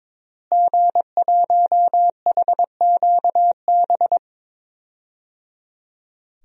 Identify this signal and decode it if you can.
Morse code, heard as a keyed tone, about 22 wpm, 700 Hz. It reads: G1HQB
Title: DXCC-100 Call Signs - 5 Characters - Repeated 3x - 22wpm